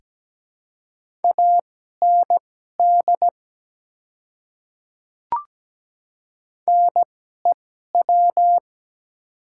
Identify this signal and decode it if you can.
Morse code, heard as a keyed tone, about 17 wpm, spaced as for 12 wpm, 700 Hz. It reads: AND NEW
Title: 2x Character Spacing - Top 200 Words - 17wpm